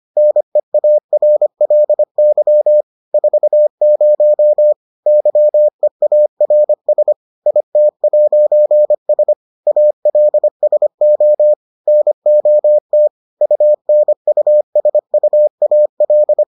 Morse code, heard as a keyed tone, about 25 wpm, 600 Hz. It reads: NEARLY 40 YEARS IT'S ALSO NOT UNUSUAL